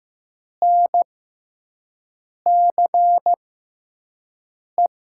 Morse code, heard as a keyed tone, about 15 wpm, spaced as for 5 wpm, 700 Hz. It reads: NCE
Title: USA Call Signs - 6 Characters - 6x Character Spacing - 15wpm